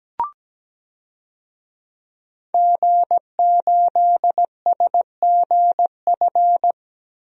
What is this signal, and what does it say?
Morse code, heard as a keyed tone, about 17 wpm, 700 Hz. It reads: G8SGF
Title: DXCC-100 Call Signs - 5 Characters - 17wpm